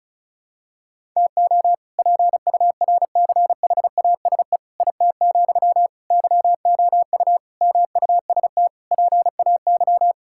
Morse code, heard as a keyed tone, about 34 wpm, 700 Hz. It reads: TO PURCHASE IT, YOU MUST PAY
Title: Sentences from Top 1000 Words - Review Part 5 - 35wpm